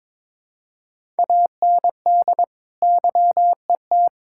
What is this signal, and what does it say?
Morse code, heard as a keyed tone, about 22 wpm, 700 Hz. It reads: AND YET
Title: Sentences from Top 1000 Words - Review Part 6 - 22wpm